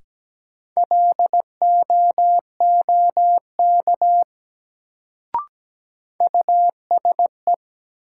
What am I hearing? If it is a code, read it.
Morse code, heard as a keyed tone, about 17 wpm, 700 Hz. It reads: LOOK USE
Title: Sentences from Top 100 Words - New Word Drill - 17wpm